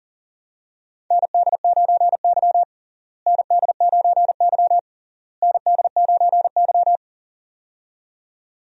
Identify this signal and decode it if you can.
Morse code, heard as a keyed tone, about 40 wpm, 700 Hz. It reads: ND9Y ND9Y ND9Y
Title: USA Call Signs - 4 Characters - Repeated 3x - 40wpm